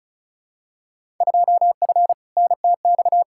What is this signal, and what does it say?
Morse code, heard as a keyed tone, about 36 wpm, 700 Hz. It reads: 2F NTX
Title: Field Day Exchanges - 35wpm